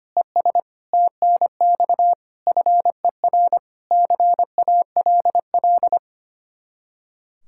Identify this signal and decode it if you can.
Morse code, heard as a keyed tone, about 30 wpm, 700 Hz. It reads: ES TNX FER CALL
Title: Speed-Racing Common QSO Phrases - 25wpm